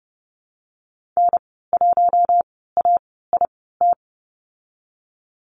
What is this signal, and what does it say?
Morse code, heard as a keyed tone, about 30 wpm, spaced as for 17 wpm, 700 Hz. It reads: N1AIT
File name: Lesson 14g - Number 2 - Call Signs 30wpm